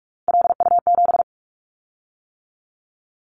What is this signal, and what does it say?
Morse code, heard as a keyed tone, about 46 wpm, 700 Hz. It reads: LU7
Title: Callsign Prefixes for Top 100 DXCC - 45wpm